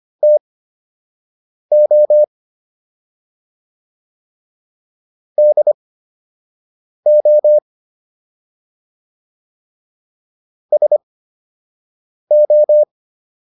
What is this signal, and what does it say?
Morse code, heard as a keyed tone, about 25 wpm, spaced as for 6 wpm, 600 Hz. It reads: TO DO SO